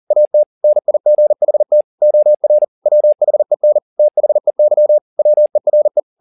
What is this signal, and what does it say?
Morse code, heard as a keyed tone, about 40 wpm, 600 Hz. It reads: AT NIGHT OR WHEN THEY WERE